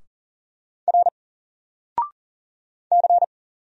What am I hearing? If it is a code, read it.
Morse code, heard as a keyed tone, about 41 wpm, 700 Hz. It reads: R C
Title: Single Letter-Number Alt 40wpm